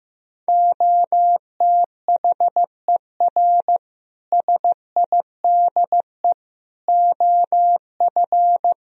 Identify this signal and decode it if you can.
Morse code, heard as a keyed tone, about 16 wpm, 700 Hz. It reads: OTHER SIDE OF
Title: Speed-Racing - Sets of 4 Words 15wpm